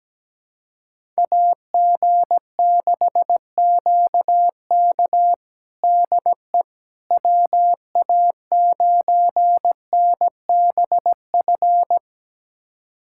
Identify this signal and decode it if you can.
Morse code, heard as a keyed tone, about 17 wpm, 700 Hz. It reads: AG6QK DE WA9NBF
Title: USA Callsign DE Callsign 17wpm